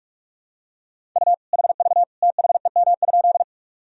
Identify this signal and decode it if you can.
Morse code, heard as a keyed tone, about 46 wpm, 700 Hz. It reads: U HV THEM?
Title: Sentences from Top 100 Words and Abbreviations 45wpm